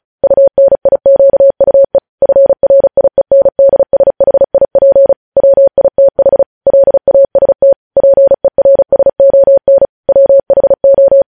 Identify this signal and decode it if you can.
Morse code, heard as a keyed tone, about 36 wpm, 550 Hz. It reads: UNIQUE FRIENDSHIP WITH LAST PERSON WHO
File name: podcast / News headlines-1x-35x35